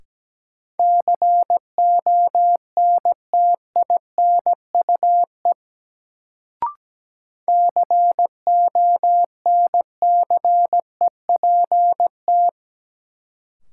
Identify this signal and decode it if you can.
Morse code, heard as a keyed tone, about 17 wpm, 700 Hz. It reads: CONTINUE CONCEPT
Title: Sentences from Top 1100 Words - New Word Drill Alt - 17wpm